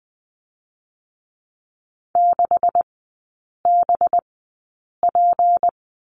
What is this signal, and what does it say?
Morse code, heard as a keyed tone, about 20 wpm, spaced as for 8 wpm, 700 Hz. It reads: 6BP
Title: 184 Lesson 30e - Letter b - Three Characters 20wpm